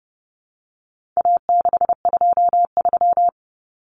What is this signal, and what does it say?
Morse code, heard as a keyed tone, about 30 wpm, 700 Hz. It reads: A623
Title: Common Cut Numbers - 4-Digit 30wpm